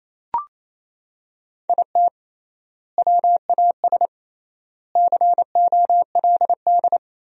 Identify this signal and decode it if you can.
Morse code, heard as a keyed tone, about 28 wpm, 700 Hz. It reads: IT WAS COLD
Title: 3x Word-Spacing - Sentences from Top 500 Words - 28wpm